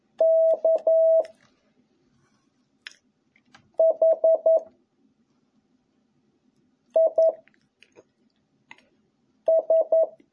Morse code, heard as a keyed tone, about 11 wpm, 660 Hz. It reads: K H I S